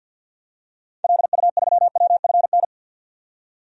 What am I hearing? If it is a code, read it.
Morse code, heard as a keyed tone, about 51 wpm, 700 Hz. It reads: LU3WFN
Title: DXCC-100 Call Signs - 6 Characters - 50wpm